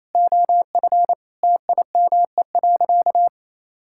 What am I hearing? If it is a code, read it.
Morse code, heard as a keyed tone, about 27 wpm, 700 Hz. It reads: OF TIME.
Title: Sentences from Top 1000 Words - Review Part 5 - 28wpm